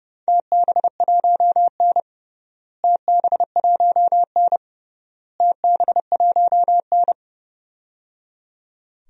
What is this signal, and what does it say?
Morse code, heard as a keyed tone, about 30 wpm, 700 Hz. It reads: TB1N TB1N TB1N
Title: DXCC-100 Call Signs - 4 Characters - Repeated 3x - 30wpm